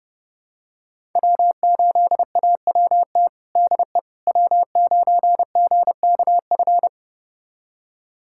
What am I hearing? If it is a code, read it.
Morse code, heard as a keyed tone, about 30 wpm, 700 Hz. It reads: W8AWT DE W9GKF
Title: USA Callsign DE Callsign - 5 Characters - 30wpm